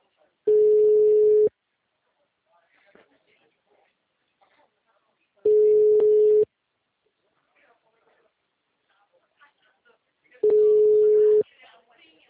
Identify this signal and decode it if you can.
Ringback tone (Europe, 425 Hz), with speech.